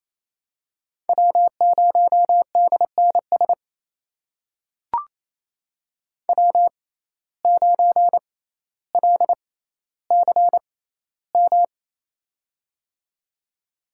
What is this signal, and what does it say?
Morse code, heard as a keyed tone, about 28 wpm, spaced as for 14 wpm, 700 Hz. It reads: W0DNS W9LCM
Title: USA Call Signs - 5 Characters - 6x Character Spacing - 28wpm